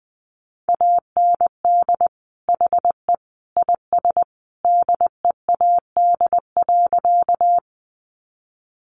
Morse code, heard as a keyed tone, about 20 wpm, 700 Hz. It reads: AND HE IS DEAD.
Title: Sentences from Top 1500 Words 20wpm